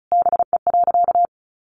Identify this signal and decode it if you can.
Morse code, heard as a keyed tone, about 34 wpm, 700 Hz. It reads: BE.
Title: A Sound Of Thunder 35wpm